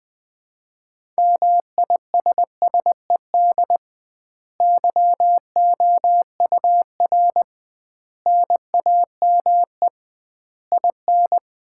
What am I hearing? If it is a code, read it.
Morse code, heard as a keyed tone, about 21 wpm, 700 Hz. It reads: MISSED YOUR NAME IN
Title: Common QSO Phrases - 2x Word-Spacing - 20wpm